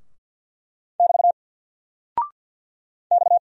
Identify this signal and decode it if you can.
Morse code, heard as a keyed tone, about 48 wpm, 700 Hz. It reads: <BT> X
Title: Single Character (Letter:Number:Punctuation:Prosign) Alt 50wpm